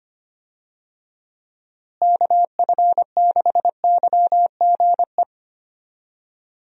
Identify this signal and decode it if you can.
Morse code, heard as a keyed tone, about 25 wpm, 700 Hz. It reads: KF6YGE
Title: USA Call Signs - 6 Characters - Repeated 2x - 25wpm